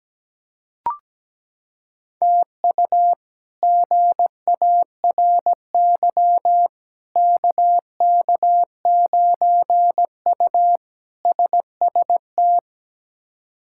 Morse code, heard as a keyed tone, about 17 wpm, 700 Hz. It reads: TU GARY KK9U SST
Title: K1USN SST QSO Practice - 17wpm